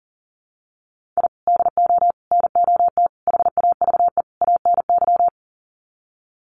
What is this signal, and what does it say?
Morse code, heard as a keyed tone, about 40 wpm, 700 Hz. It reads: I DO NOT HAVE ANY
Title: Sets of 5 Words 40wpm